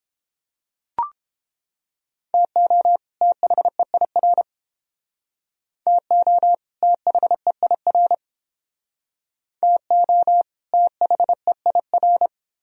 Morse code, heard as a keyed tone, about 29 wpm, 700 Hz. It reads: TO THEIR TO THEIR TO THEIR